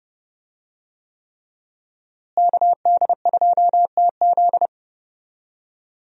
Morse code, heard as a keyed tone, about 32 wpm, 700 Hz. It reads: KD2TZ
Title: USA Call Signs - 5 Characters - Repeated 3x - 30wpm